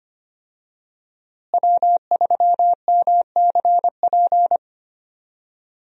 Morse code, heard as a keyed tone, about 25 wpm, 700 Hz. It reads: W3MCP
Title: USA Call Signs - 5 Characters - Repeated 3x - 25wpm